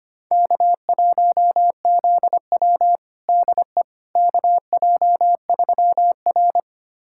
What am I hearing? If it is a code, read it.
Morse code, heard as a keyed tone, about 25 wpm, 700 Hz. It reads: K1ZW DE KJ3R
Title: USA Callsign DE Callsign - 4 Characters - 25wpm